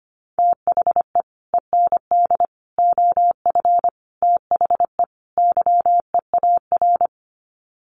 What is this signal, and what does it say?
Morse code, heard as a keyed tone, about 25 wpm, 700 Hz. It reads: THE END OF THE YEAR
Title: Sets of 5 Words 25wpm